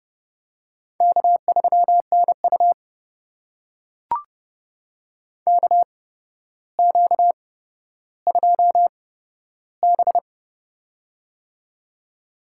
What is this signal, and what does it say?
Morse code, heard as a keyed tone, about 30 wpm, spaced as for 13 wpm, 700 Hz. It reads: K3NU KQ2B
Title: USA Call Signs - 4 Characters - 8x Character Spacing - 30wpm